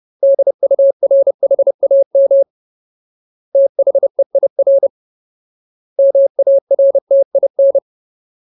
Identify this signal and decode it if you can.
Morse code, heard as a keyed tone, about 30 wpm, 550 Hz. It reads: DURHAM THEIR MARTIN
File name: Ultimate Word List / 30WPM Ultimate